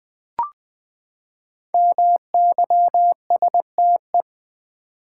Morse code, heard as a keyed tone, about 20 wpm, 700 Hz. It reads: MYSTE
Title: Build A Word - 5 Letters At A Time - 10-Letter Words - 20wpm